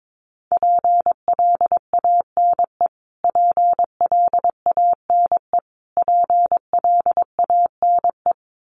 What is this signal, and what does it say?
Morse code, heard as a keyed tone, about 22 wpm, 700 Hz. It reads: PLANE PLANE PLANE
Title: Top 500 Words Repeated 22wpm